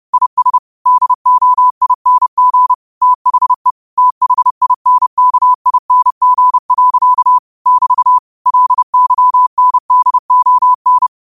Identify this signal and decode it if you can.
Morse code, heard as a keyed tone, about 30 wpm, 1000 Hz. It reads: IS DOING THE THINKING. <BT> LYNDON
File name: qotdfiles / 30wpm